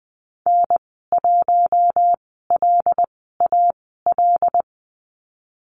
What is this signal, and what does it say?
Morse code, heard as a keyed tone, about 20 wpm, spaced as for 15 wpm, 700 Hz. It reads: N1LAL
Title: 088 Lesson 15g - Number 5 - Call Signs 20wpm